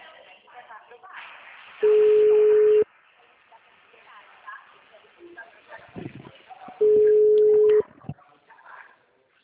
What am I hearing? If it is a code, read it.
Ringback tone (Europe, 425 Hz), with speech.